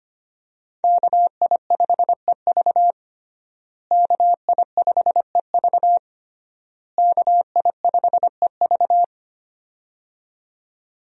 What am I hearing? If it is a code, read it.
Morse code, heard as a keyed tone, about 25 wpm, 700 Hz. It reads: KI5EV KI5EV KI5EV
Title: USA Call Signs - 5 Characters - Repeated 3x - 25wpm